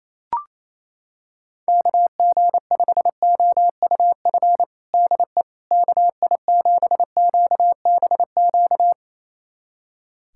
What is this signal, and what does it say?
Morse code, heard as a keyed tone, about 28 wpm, 700 Hz. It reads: KG5OUF DE KI7QBQ
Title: USA Callsign DE Callsign 28wpm